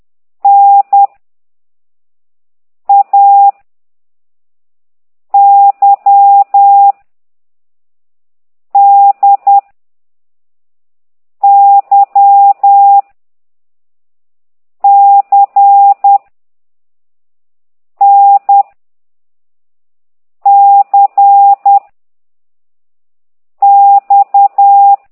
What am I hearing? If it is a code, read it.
Morse code, heard as a keyed tone, about 10 wpm, 800 Hz. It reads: N A Y D Y C N C X
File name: ct2idw / exerc9